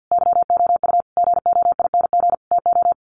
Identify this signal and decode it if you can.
Morse code, heard as a keyed tone, about 50 wpm, 700 Hz. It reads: YOU GOING TO
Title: Sets of 5 Words 50wpm